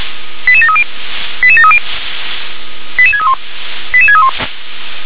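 Five-tone selective call (ZVEI), heard as quick runs of five tones, tones of about 70 ms. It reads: 88730 88730 88631 88631
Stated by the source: ZVEI five-tone call received on SDR.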